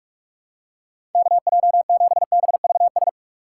Morse code, heard as a keyed tone, about 46 wpm, 700 Hz. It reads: KJ7BVS
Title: USA Call Signs - 6 Characters - 4x Character Spacing - 45wpm